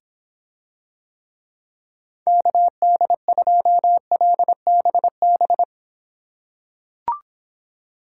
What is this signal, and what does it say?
Morse code, heard as a keyed tone, about 26 wpm, 700 Hz. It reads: KD2LBB